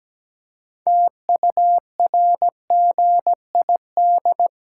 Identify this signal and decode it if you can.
Morse code, heard as a keyed tone, about 17 wpm, 700 Hz. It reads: TURGID
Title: Pangram Challenge v2 - 17wpm